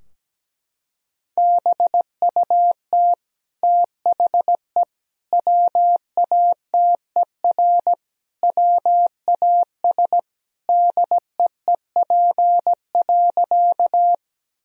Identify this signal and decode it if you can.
Morse code, heard as a keyed tone, about 17 wpm, 700 Hz. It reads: BUT THE WATER WAS DEEP.
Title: Sentences from Top 1000 Words - Review Part 3 - 17wpm